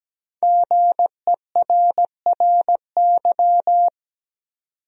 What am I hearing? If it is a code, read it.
Morse code, heard as a keyed tone, about 17 wpm, 700 Hz. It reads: GERRY
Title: USA Common Names - Max 5 Letters - 17wpm